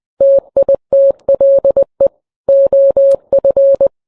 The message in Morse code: TITLE OF